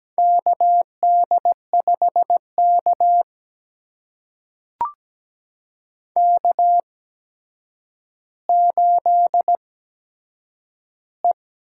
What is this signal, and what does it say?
Morse code, heard as a keyed tone, about 17 wpm, spaced as for 8 wpm, 700 Hz. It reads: KD5K K8E
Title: USA Call Signs - 4 Characters - 8x Character Spacing - 17wpm